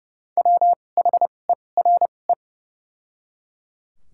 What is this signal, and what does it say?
Morse code, heard as a keyed tone, about 30 wpm, spaced as for 22 wpm, 700 Hz. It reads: WHERE
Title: 2x Character Spacing - Top 200 Words - 30wpm